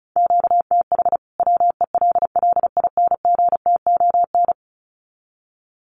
Morse code, heard as a keyed tone, about 34 wpm, 700 Hz. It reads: QTH WELLINGTON
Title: Common QSO Phrases 35wpm